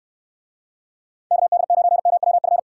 Morse code, heard as a keyed tone, about 69 wpm, 700 Hz. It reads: KD9MWF